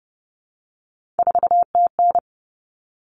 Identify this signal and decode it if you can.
Morse code, heard as a keyed tone, about 30 wpm, 700 Hz. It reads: 4TN
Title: All Cut Numbers - 3-Digit 30wpm